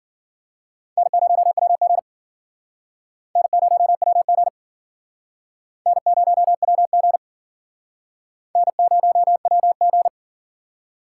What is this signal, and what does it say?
Morse code, heard as a keyed tone, about 54 wpm, 700 Hz. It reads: N0WG N0WG N0WG N0WG